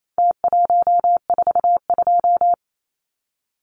Morse code, heard as a keyed tone, about 28 wpm, 700 Hz. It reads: T142
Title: Numbers - Contest Style - 4-Digit 28wpm